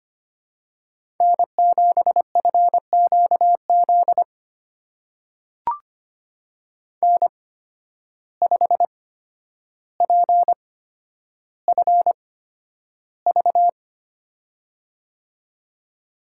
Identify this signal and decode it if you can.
Morse code, heard as a keyed tone, about 25 wpm, spaced as for 11 wpm, 700 Hz. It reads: N7FQZ N5PFV
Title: USA Call Signs - 5 Characters - 8x Character Spacing - 25wpm